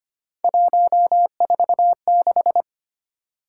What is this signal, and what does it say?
Morse code, heard as a keyed tone, about 29 wpm, 700 Hz. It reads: 146
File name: Speed-Racing - Three Letter-Number Combinations - 25wpm